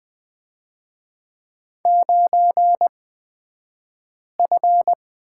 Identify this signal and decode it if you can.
Morse code, heard as a keyed tone, about 20 wpm, spaced as for 5 wpm, 700 Hz. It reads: 9F
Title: 223 Lesson 36d - BT - Two Characters 20wpm